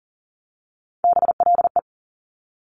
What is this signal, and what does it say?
Morse code, heard as a keyed tone, about 41 wpm, 690 Hz. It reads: BLE
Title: Top Three Letter Combinations in Words 40wpm